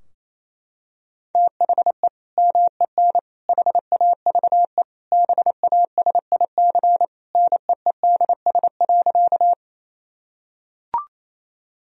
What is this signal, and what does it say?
Morse code, heard as a keyed tone, about 27 wpm, 700 Hz. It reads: THE MEN HAVE BASIC NEEDS.
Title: Sentences from Top 1000 Words - Review Part 3 - 28wpm